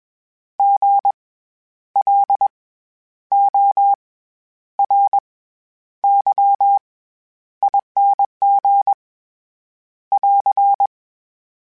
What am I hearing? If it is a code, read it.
Morse code, heard as a keyed tone, about 21 wpm, 800 Hz. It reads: G L O R Y ING <AR>
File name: Special INGs-mp3s / Special INGs-mp3s / SlowToFaster ing @20